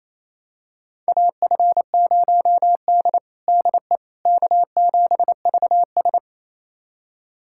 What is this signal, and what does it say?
Morse code, heard as a keyed tone, about 28 wpm, 700 Hz. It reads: AF0D DE K7VS
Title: USA Callsign DE Callsign - 4 Characters - 28wpm